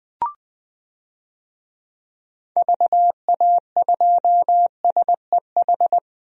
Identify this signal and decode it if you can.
Morse code, heard as a keyed tone, about 20 wpm, 700 Hz. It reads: VA2SEH
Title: DXCC-100 Call Signs - 6 Characters - 20wpm